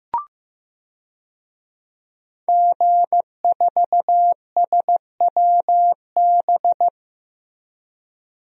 Morse code, heard as a keyed tone, about 15 wpm, 700 Hz. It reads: G4SWB
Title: DXCC-100 Call Signs - 5 Characters - Repeated 2x - 15wpm